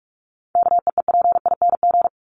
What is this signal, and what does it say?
Morse code, heard as a keyed tone, about 44 wpm, 700 Hz. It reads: KEEPING
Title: ING Suffix - Top 100 Words with Base Words 45wpm